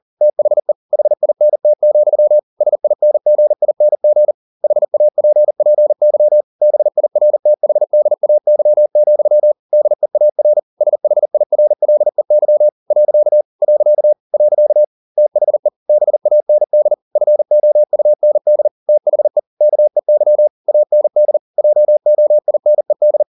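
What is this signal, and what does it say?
Morse code, heard as a keyed tone, about 40 wpm, 600 Hz. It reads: THE HINT, SINGING HAPPY BIRTHDAY, DEAR SHIRLEY . . THE BAND FOUND THE KEY AND JOINED
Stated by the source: Morse code at 40 wpm, practice text, CW med Gnister.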